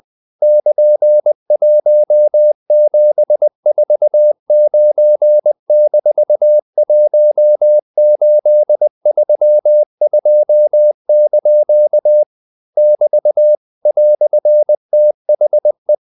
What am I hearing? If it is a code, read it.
Morse code, heard as a keyed tone, about 20 wpm, 600 Hz. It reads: <KN>1749-1832) <BT> "THE
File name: CW / Practice / EnglishQuotations0320